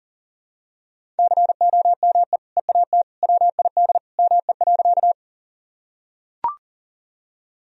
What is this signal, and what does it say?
Morse code, heard as a keyed tone, about 40 wpm, 700 Hz. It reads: COME EAT WID ME.
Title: Sentences from Top 300 Words and Abbreviations 40wpm